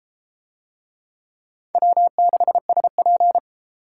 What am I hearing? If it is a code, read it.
Morse code, heard as a keyed tone, about 30 wpm, 700 Hz. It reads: W6SP